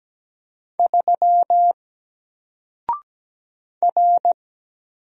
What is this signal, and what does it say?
Morse code, heard as a keyed tone, about 17 wpm, 700 Hz. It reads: 3 R
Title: Single Character (Letter:Number:Punctuation:Prosign) 17wpm